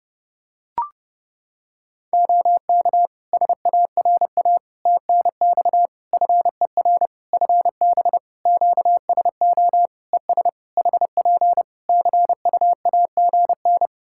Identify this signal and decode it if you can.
Morse code, heard as a keyed tone, about 30 wpm, 700 Hz. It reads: OK SARA TNX FER FB QSO ES HP CUAGN
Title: Practice QSO - Exchange 4 - Ending - 30wpm